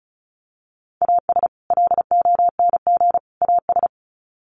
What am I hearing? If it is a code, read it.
Morse code, heard as a keyed tone, about 34 wpm, 700 Hz. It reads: AS LONG AS
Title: Sets of 3 Words 35wpm